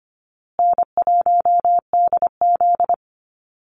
Morse code, heard as a keyed tone, about 25 wpm, 700 Hz. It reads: N1DZ
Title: USA Call Signs Repeated 2x 25wpm